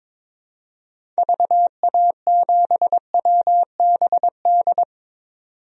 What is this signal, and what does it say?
Morse code, heard as a keyed tone, about 22 wpm, 700 Hz. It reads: VA7WBD
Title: DXCC-100 Call Signs - 6 Characters - 22wpm